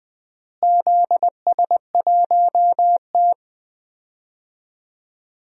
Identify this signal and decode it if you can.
Morse code, heard as a keyed tone, about 20 wpm, 700 Hz. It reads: ZS1T
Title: DXCC-100 Call Signs - 4 Characters - Repeated 3x - 20wpm